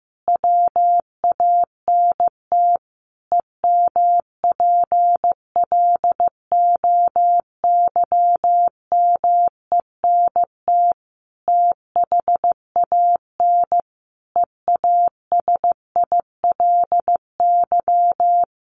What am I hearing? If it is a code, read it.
Morse code, heard as a keyed tone, about 15 wpm, 700 Hz. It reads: WANT EMPLOYMENT THAN EASILY
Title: Sentences from Top 1500 Words 15wpm